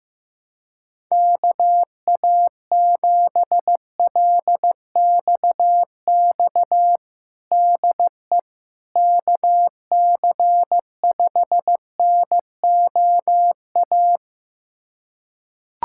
Morse code, heard as a keyed tone, about 15 wpm, 700 Hz. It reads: KA7LXX DE KC5NOA
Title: USA Callsign DE Callsign 15wpm